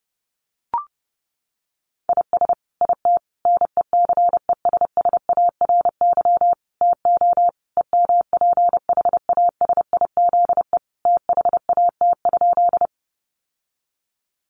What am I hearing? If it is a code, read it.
Morse code, heard as a keyed tone, about 30 wpm, 700 Hz. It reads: IS IT NECESSARY TO EMPHASIZE THAT?
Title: Sentences from Top 1500 Words 30wpm